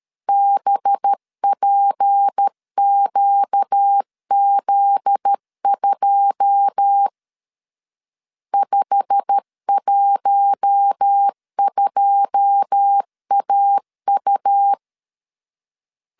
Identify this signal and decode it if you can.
Morse code, heard as a keyed tone, about 13 wpm, 800 Hz. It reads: BPQZ2 512AU
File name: LEZIONI / 9 2 12wpm 2301